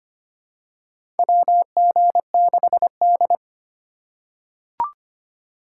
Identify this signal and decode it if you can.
Morse code, heard as a keyed tone, about 25 wpm, 700 Hz. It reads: WG6D
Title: USA Call Signs - 4 Characters - 4x Character Spacing - 25wpm